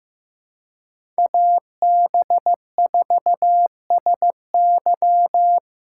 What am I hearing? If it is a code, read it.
Morse code, heard as a keyed tone, about 15 wpm, 700 Hz. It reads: AB4SY
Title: USA Callsign DE Callsign - 5 Characters - 15wpm